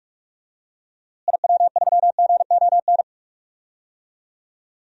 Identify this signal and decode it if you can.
Morse code, heard as a keyed tone, about 46 wpm, 700 Hz. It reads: IW3GON
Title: DXCC-100 Call Signs - 6 Characters - Repeated 3x - 45wpm